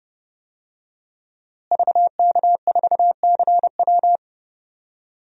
Morse code, heard as a keyed tone, about 32 wpm, 700 Hz. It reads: VK4CW
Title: DXCC-100 Call Signs - 5 Characters - Repeated 2x - 30wpm